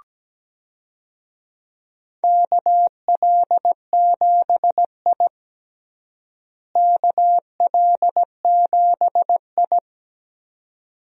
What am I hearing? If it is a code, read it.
Morse code, heard as a keyed tone, about 17 wpm, 700 Hz. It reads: KL7I KL7I
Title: DXCC-100 Call Signs - 4 Characters - Repeated 2x - 17wpm